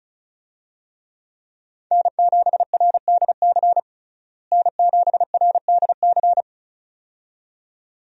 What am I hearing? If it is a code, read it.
Morse code, heard as a keyed tone, about 34 wpm, 700 Hz. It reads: N7RDC N7RDC
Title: USA Call Signs - 5 Characters - Repeated 2x - 35wpm